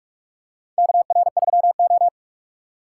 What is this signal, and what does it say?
Morse code, heard as a keyed tone, about 45 wpm, 700 Hz. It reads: KR3O
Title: USA Call Signs - 4 Characters - 3x Character Spacing - 45wpm